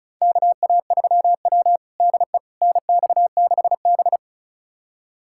Morse code, heard as a keyed tone, about 36 wpm, 700 Hz. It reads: KA3W DE NX6B
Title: USA Callsign DE Callsign - 4 Characters - 35wpm